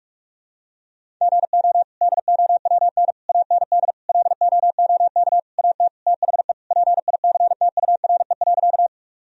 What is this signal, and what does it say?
Morse code, heard as a keyed tone, about 46 wpm, 700 Hz. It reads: GO DOWN AND LOOK AT THE PICTURE.
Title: Sentences from Top 1000 Words - Review Part 1 - 45wpm